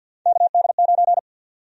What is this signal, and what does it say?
Morse code, heard as a keyed tone, about 51 wpm, 700 Hz. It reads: KD9
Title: USA Call Sign Prefixes - 50wpm